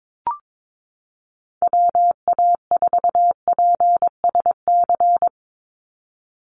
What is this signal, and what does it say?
Morse code, heard as a keyed tone, about 22 wpm, 700 Hz. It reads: WA4PSC
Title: Call Signs - 22wpm